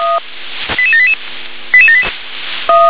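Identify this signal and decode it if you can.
Five-tone selective call (ZVEI), heard as quick runs of five tones, tones of about 70 ms. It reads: 88788
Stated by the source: ZVEI five-tone selective call 88761 received on SDR.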